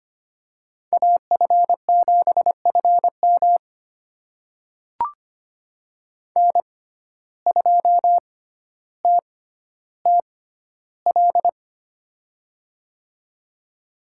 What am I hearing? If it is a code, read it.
Morse code, heard as a keyed tone, about 25 wpm, spaced as for 13 wpm, 700 Hz. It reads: AF7FM N2TTL
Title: USA Call Signs - 5 Characters - 6x Character Spacing - 25wpm